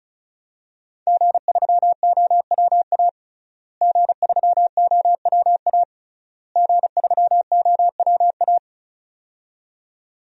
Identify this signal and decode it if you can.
Morse code, heard as a keyed tone, about 36 wpm, 700 Hz. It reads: G3OWA G3OWA G3OWA
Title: DXCC-100 Call Signs - 5 Characters - Repeated 3x - 35wpm